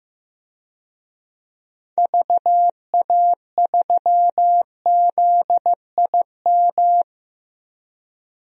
Morse code, heard as a keyed tone, about 16 wpm, 700 Hz. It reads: VA3ZIM
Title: DXCC-100 Call Signs - 6 Characters - 15wpm